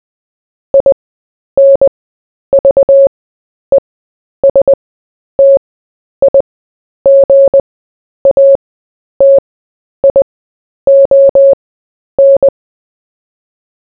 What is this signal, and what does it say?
Morse code, heard as a keyed tone, about 20 wpm, spaced as for 10 wpm, 550 Hz. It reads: INVESTIGATION
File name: cw / fists / Short-Story-4-10wpm